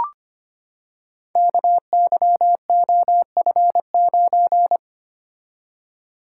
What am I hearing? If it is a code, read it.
Morse code, heard as a keyed tone, about 25 wpm, 700 Hz. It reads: KYOF9
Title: Code Groups - 5 Letter:Numbers - 25wpm